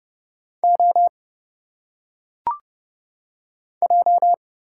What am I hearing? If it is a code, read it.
Morse code, heard as a keyed tone, about 29 wpm, 700 Hz. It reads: O J
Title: Single Letters - Phonetic Pronunciation 30wpm